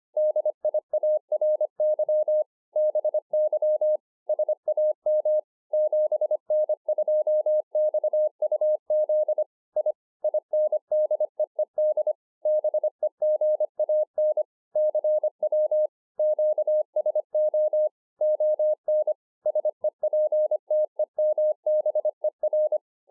Morse code, heard as a keyed tone, about 25 wpm, 610 Hz. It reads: DIARY BY SAM 7N2XUZ I INDEED BEGAN CW QSO ON SEPTEMBER
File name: 2011-04-mp3 / BEGINNERS DIARY 25WPM